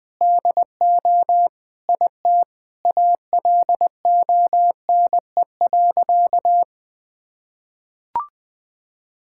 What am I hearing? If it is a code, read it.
Morse code, heard as a keyed tone, about 21 wpm, 700 Hz. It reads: DO IT ALONE.
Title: Sentences with 7-Word Phrases - Spoken Beginning - 20wpm